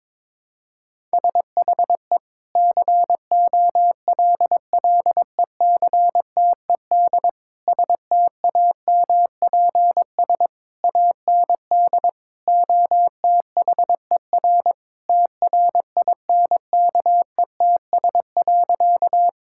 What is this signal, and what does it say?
Morse code, heard as a keyed tone, about 22 wpm, 700 Hz. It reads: SHE COLLECTED STAMPS AND OTHER TRINKETS.
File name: practice / 22wpm / Sentences with 2-Word Phrases - 22wpm